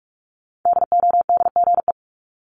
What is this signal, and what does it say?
Morse code, heard as a keyed tone, about 46 wpm, 700 Hz. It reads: DODGE
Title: Multi-letter Phonemes Repeated 3x 45wpm